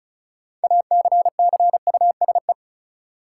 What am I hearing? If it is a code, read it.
Morse code, heard as a keyed tone, about 36 wpm, 700 Hz. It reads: ACCUSE
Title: Six Letter Words - 35wpm